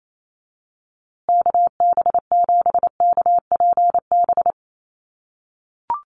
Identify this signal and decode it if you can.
Morse code, heard as a keyed tone, about 28 wpm, 700 Hz. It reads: KB7KPB
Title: USA Call Signs Repeated 3x 28wpm